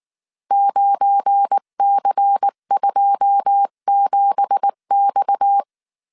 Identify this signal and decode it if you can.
Morse code, heard as a keyed tone, about 19 wpm, 800 Hz. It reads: 9C27X
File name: CW / DET / 1 9 1500C S B 18wpm 2551